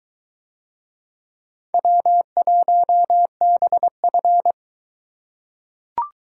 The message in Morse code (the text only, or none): W1BF